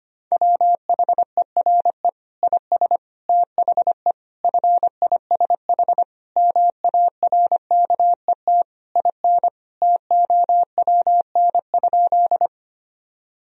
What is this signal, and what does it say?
Morse code, heard as a keyed tone, about 25 wpm, 700 Hz. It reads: WHERE IS THE FISH MARKET IN TOWN?
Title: Sentences from Top 1000 Words - Review Part 3 - 25wpm